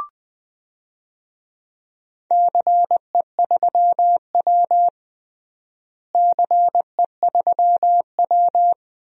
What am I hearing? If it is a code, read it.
Morse code, heard as a keyed tone, about 20 wpm, 700 Hz. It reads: CE3W CE3W
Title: DXCC-100 Call Signs - 4 Characters - Repeated 3x - 20wpm